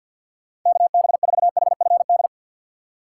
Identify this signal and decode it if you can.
Morse code, heard as a keyed tone, about 52 wpm, 700 Hz. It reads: KB4HFD
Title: USA Call Signs - 6 Characters - 8x Character Spacing - 50wpm